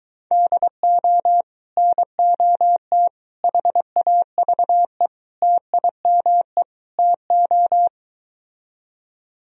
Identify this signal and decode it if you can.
Morse code, heard as a keyed tone, about 23 wpm, 700 Hz. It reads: DO NOT HAVE TIME TO